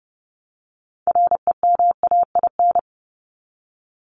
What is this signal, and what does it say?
Morse code, heard as a keyed tone, about 30 wpm, 700 Hz. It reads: REMAIN
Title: ES:S Suffix - Top 100 Words with Base Words 30wpm